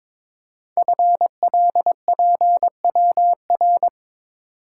Morse code, heard as a keyed tone, about 22 wpm, 700 Hz. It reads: FLPWR
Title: Code Groups - 5 Letters - 22wpm